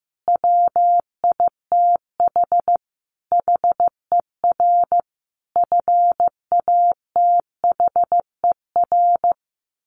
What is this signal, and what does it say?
Morse code, heard as a keyed tone, about 14 wpm, 700 Hz. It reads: WITH HER FATHER
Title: Sentences from Top 1100 Words 15wpm